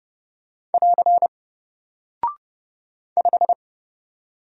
Morse code, heard as a keyed tone, about 30 wpm, 700 Hz. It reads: <AR> 5
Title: Single Character (Letter:Number:Punctuation:Prosign) 30wpm